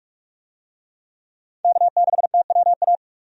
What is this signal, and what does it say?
Morse code, heard as a keyed tone, about 47 wpm, 700 Hz. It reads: K6TWA
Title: USA Call Signs - 5 Characters - 45wpm